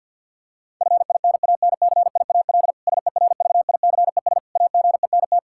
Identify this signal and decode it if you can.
Morse code, heard as a keyed tone, about 47 wpm, 700 Hz. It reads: FINANCIAL SERVICES AGENT
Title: Practice QSO - Exchange 3 - More Info - 50wpm